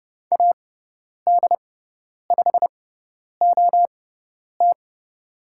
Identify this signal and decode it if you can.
Morse code, heard as a keyed tone, about 30 wpm, spaced as for 10 wpm, 700 Hz. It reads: AD5OT
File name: Lesson 17g - Letter u - Call Signs 30wpm